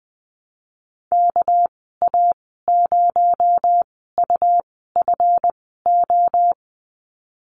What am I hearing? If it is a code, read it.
Morse code, heard as a keyed tone, about 20 wpm, spaced as for 15 wpm, 700 Hz. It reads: KA0UFO
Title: 231 Lesson 37g - Letter x - Call Signs 20wpm